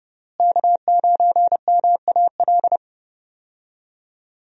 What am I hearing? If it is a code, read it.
Morse code, heard as a keyed tone, about 30 wpm, 700 Hz. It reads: K9MAL
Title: USA Call Signs - 5 Characters - Repeated 3x - 30wpm